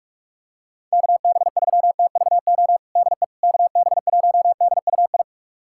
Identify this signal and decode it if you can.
Morse code, heard as a keyed tone, about 46 wpm, 700 Hz. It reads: KB3TVO DE KB1DUI
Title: USA Callsign DE Callsign - 6 Characters - 45wpm